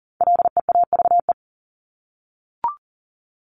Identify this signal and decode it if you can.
Morse code, heard as a keyed tone, about 40 wpm, 700 Hz. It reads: LEAVE
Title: Top 500 Words Repeated 40wpm